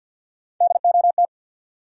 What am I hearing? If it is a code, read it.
Morse code, heard as a keyed tone, about 48 wpm, 700 Hz. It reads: DOT
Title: USA Common Names - Max 4 Letters - 50wpm